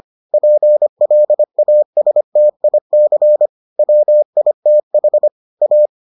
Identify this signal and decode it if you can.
Morse code, heard as a keyed tone, about 25 wpm, 600 Hz. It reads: PLASTIC WITH A